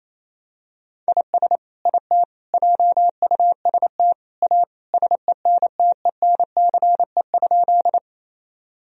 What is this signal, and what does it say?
Morse code, heard as a keyed tone, about 28 wpm, 700 Hz. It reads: IS IT JUST A SENTENCE?
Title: Sentences from Top 1000 Words - Review Part 1 - 28wpm